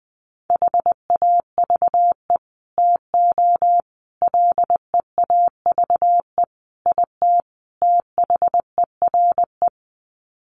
Morse code, heard as a keyed tone, about 20 wpm, 700 Hz. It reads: HAVE TO LEAVE IT THERE
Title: Sets of 5 Words Encore2 20wpm